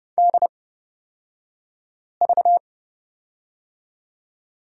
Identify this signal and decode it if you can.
Morse code, heard as a keyed tone, about 30 wpm, spaced as for 5 wpm, 700 Hz. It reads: DV
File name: Lesson 38d - Letter q - Two Characters 30wpm